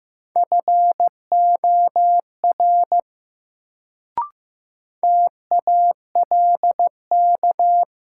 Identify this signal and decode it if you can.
Morse code, heard as a keyed tone, about 15 wpm, 700 Hz. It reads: FOR TALK
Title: Sets of 2 Words - New Word Drill - 15wpm